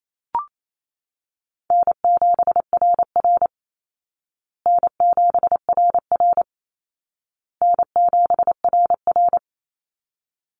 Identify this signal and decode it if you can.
Morse code, heard as a keyed tone, about 28 wpm, 700 Hz. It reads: N7RR N7RR N7RR
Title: Call Signs Repeated 2x - 28wpm